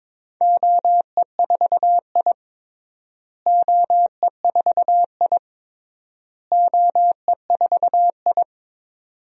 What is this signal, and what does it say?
Morse code, heard as a keyed tone, about 22 wpm, 700 Hz. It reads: OE4I OE4I OE4I
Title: DXCC-100 Call Signs - 4 Characters - Repeated 3x - 22wpm